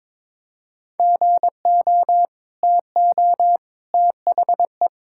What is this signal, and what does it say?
Morse code, heard as a keyed tone, about 21 wpm, 700 Hz. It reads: GO TO THE
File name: practice / 22wpm / Sentences with 2-Word Phrases - 22wpm